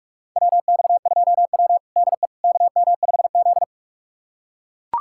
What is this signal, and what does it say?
Morse code, heard as a keyed tone, about 42 wpm, 700 Hz. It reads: WX2W DE KM5Z
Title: USA Callsign DE Callsign - 4 Characters - 45wpm